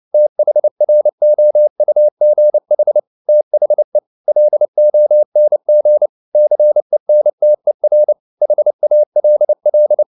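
Morse code, heard as a keyed tone, about 29 wpm, 600 Hz. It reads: THROUGH THE LONG CENTER HALL
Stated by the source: Morse code at 29 wpm, sending Ken Follett's novel Never in English, from CW med Gnister.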